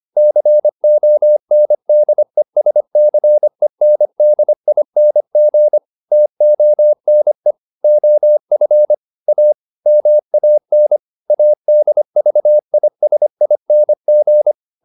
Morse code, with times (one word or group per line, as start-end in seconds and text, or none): 0.2-5.8 CONDESCENDING
6.1-7.5 TONE
7.8-8.9 OF
9.3-9.5 A
9.9-11.0 MAN
11.3-14.5 ADVISING